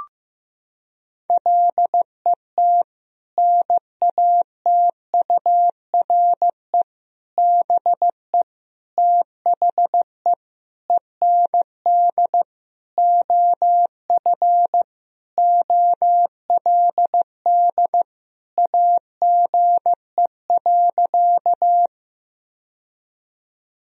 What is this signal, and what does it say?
Morse code, heard as a keyed tone, about 15 wpm, 700 Hz. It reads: LET NATURE BE THE END OF OLD AGE.
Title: Sentences from Top 1000 Words - Review Part 3 - 15wpm